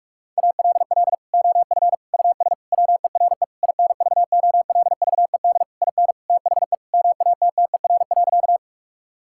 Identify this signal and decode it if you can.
Morse code, heard as a keyed tone, about 46 wpm, 700 Hz. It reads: ALL OF US WERE INVOLVED IN THE MATTER.
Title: Sentences from Top 1000 Words - Review Part 5 - 45wpm